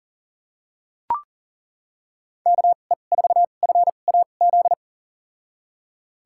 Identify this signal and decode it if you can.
Morse code, heard as a keyed tone, about 40 wpm, spaced as for 29 wpm, 700 Hz. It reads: KE4FAZ
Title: USA Call Signs - 2x Character Spacing - 40wpm